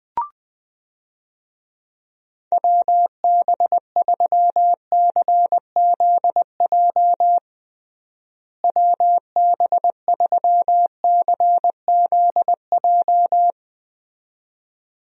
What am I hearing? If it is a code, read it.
Morse code, heard as a keyed tone, about 21 wpm, 700 Hz. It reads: WB3CZJ WB3CZJ
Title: USA Call Signs - 6 Characters - Repeated 2x - 20wpm